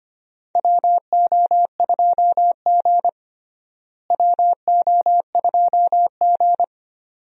Morse code, heard as a keyed tone, about 25 wpm, 700 Hz. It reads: WO2G WO2G
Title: USA Call Signs - 4 Characters - Repeated 2x - 25wpm